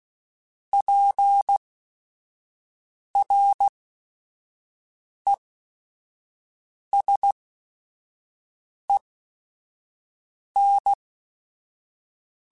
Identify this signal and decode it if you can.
Morse code, heard as a keyed tone, about 16 wpm, spaced as for 5 wpm, 800 Hz. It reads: PRESEN
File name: Technical Subjects / Technical Subjects-mp3s / XL @5